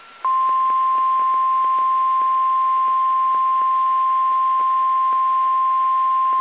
Emergency Alert System (SAME) tones (the 1050 Hz attention tone). It is heard from 0.2 to 5.4 s.
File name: eas / KSDZKDJL / KSDZKDJL